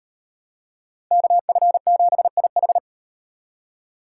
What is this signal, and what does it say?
Morse code, heard as a keyed tone, about 35 wpm, 700 Hz. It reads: KF7IH